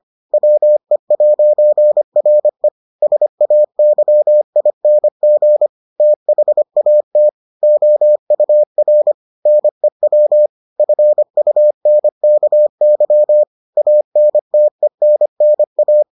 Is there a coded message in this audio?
Yes: Morse code from 0.3 to 16.0 s, heard as a keyed tone, about 25 wpm, 600 Hz. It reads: WE'RE SAYING THAT OUR NEW FUNKY ANTENNA